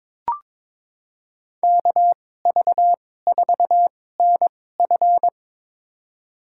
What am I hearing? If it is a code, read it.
Morse code, heard as a keyed tone, about 23 wpm, spaced as for 16 wpm, 700 Hz. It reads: KV4NF
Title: USA Call Signs - 5 Characters - 2x Character Spacing - 22wpm